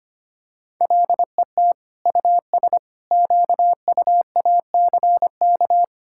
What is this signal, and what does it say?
Morse code, heard as a keyed tone, about 25 wpm, 700 Hz. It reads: LET US QUACK
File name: practice / 25wpm / All-Age Appropriate Jokes - 25wpm